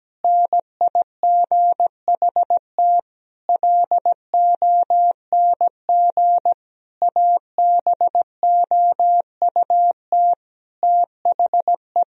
Morse code, heard as a keyed tone, about 17 wpm, 700 Hz. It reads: NIGHT LONG ABOUT THE
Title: Sentences from Top 1000 Words - Review Part 3 - 17wpm